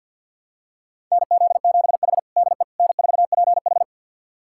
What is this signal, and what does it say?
Morse code, heard as a keyed tone, about 47 wpm, 700 Hz. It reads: NZ7H DE N4PH
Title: USA Callsign DE Callsign - 4 Characters - 50wpm